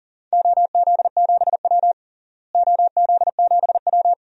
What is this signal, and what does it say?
Morse code, heard as a keyed tone, about 37 wpm, 700 Hz. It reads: OZ7W OZ7W
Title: DXCC-100 Call Signs - 4 Characters - Repeated 2x - 40wpm